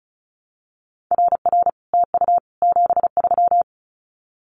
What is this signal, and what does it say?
Morse code, heard as a keyed tone, about 36 wpm, 700 Hz. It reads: RR TU 73
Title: Common QSO Phrases Encore 35wpm